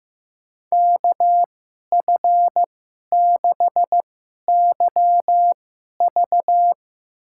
Morse code, heard as a keyed tone, about 15 wpm, spaced as for 11 wpm, 700 Hz. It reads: KF6YV
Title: USA Call Signs - 2x Character Spacing - 15wpm